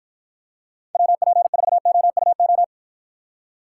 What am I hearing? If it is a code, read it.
Morse code, heard as a keyed tone, about 54 wpm, 700 Hz. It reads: WP4OUO